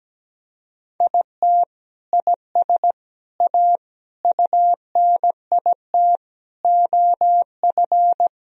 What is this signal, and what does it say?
Morse code, heard as a keyed tone, about 17 wpm, 700 Hz. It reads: IT IS A UNIT OF
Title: Sentences from Top 1000 Words - Review Part 2 - 17wpm